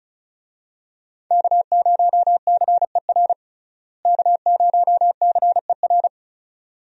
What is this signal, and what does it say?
Morse code, heard as a keyed tone, about 34 wpm, 700 Hz. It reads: K0CER K0CER
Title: USA Call Signs - 5 Characters - Repeated 2x - 35wpm